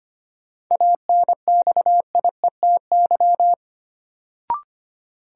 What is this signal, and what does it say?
Morse code, heard as a keyed tone, about 25 wpm, 700 Hz. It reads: ANXIETY
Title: Sentences from Top 1600 Words - New Word Drill Alt - 25wpm